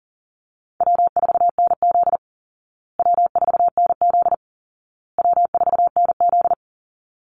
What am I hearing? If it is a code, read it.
Morse code, heard as a keyed tone, about 37 wpm, 700 Hz. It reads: W4NZ W4NZ W4NZ
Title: Easy Call Signs Repeated 3x - 40wpm